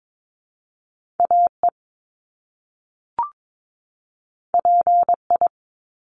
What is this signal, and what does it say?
Morse code, heard as a keyed tone, about 22 wpm, 700 Hz. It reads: AE PI
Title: Top Two Letter Combinations in Words 22wpm